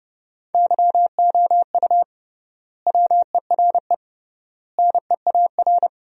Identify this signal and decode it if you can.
Morse code, heard as a keyed tone, about 32 wpm, 700 Hz. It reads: YOU WERE NEAR
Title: 3x Word-Spacing - Sentences from Top 200 Words - 30wpm